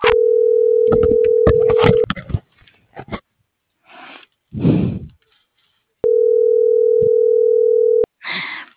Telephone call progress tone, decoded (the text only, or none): ringback tone